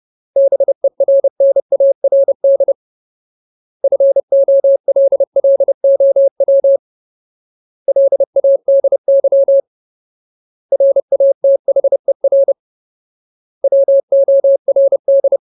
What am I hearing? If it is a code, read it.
Morse code, heard as a keyed tone, about 30 wpm, 550 Hz. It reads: BERNARD FOLLOW LADY RATHER WORD
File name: Ultimate Word List / 30WPM Ultimate